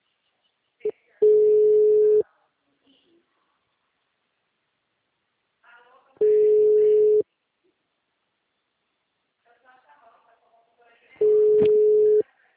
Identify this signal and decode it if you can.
Ringback tone (Europe, 425 Hz), with speech.
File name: out- -2033- - - .97948